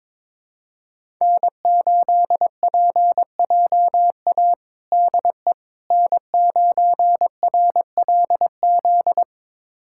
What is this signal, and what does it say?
Morse code, heard as a keyed tone, about 22 wpm, 700 Hz. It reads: N8PJA DE N9RLZ
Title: USA Callsign DE Callsign - 5 Characters - 22wpm